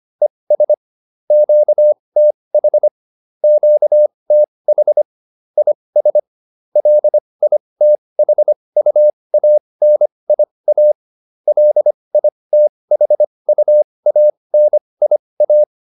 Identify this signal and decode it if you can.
Morse code, heard as a keyed tone, about 25 wpm, spaced as for 20 wpm, 600 Hz. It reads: ES QTH QTH IS LITHUANIA LITHUANIA
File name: QSO406 20